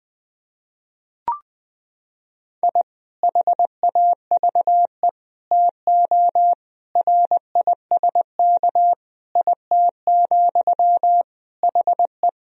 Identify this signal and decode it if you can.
Morse code, heard as a keyed tone, about 20 wpm, 700 Hz. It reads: I HAVE TO RISK IT, HE
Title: Sentences from Top 1000 Words - Review Part 4 - 20wpm